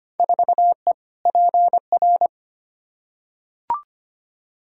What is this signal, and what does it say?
Morse code, heard as a keyed tone, about 25 wpm, 700 Hz. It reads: 4E PR
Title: Field Day Exchanges Encore - 25wpm